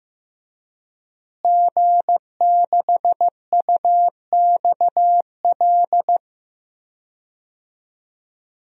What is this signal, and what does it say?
Morse code, heard as a keyed tone, about 16 wpm, 700 Hz. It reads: G6UXL
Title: DXCC-100 Call Signs - 5 Characters - Repeated 3x - 15wpm